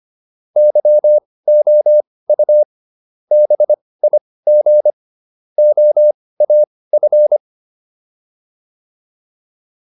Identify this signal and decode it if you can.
Morse code, heard as a keyed tone, about 25 wpm, spaced as for 18 wpm, 600 Hz. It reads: YOU BIG OAF